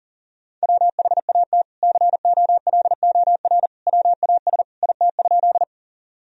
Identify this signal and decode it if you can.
Morse code, heard as a keyed tone, about 40 wpm, 700 Hz. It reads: WHAT COLOR WAS IT?
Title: Sentences from Top 1000 Words - Review Part 2 - 40wpm